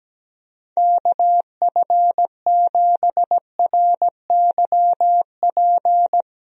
Morse code, heard as a keyed tone, about 17 wpm, 700 Hz. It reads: KF7RYP
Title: USA Call Signs - 6 Characters - Repeated 3x - 17wpm